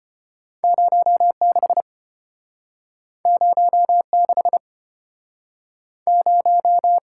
Morse code, heard as a keyed tone, about 34 wpm, 700 Hz. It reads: 06 06 0